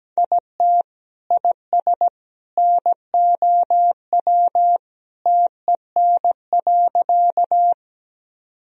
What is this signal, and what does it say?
Morse code, heard as a keyed tone, about 17 wpm, 700 Hz. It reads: IT IS NOW TEN.
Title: Sentences from Top 1000 Words - Review Part 2 - 17wpm